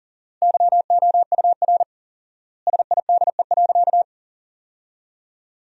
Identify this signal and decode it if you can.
Morse code, heard as a keyed tone, about 40 wpm, 700 Hz. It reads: YOUR SIDE.
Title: 4x Word-Spacing - Sentences from Top 400 Words - 40wpm